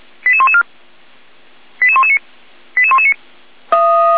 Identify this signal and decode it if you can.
Five-tone selective call (ZVEI), heard as quick runs of five tones, tones of about 70 ms. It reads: 88114 88119 88119
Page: SDR/ZVEI